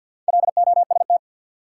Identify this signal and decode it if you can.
Morse code, heard as a keyed tone, about 53 wpm, 700 Hz. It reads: LOST
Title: Sentences from Top 1400 Words - New Word Drill - 50wpm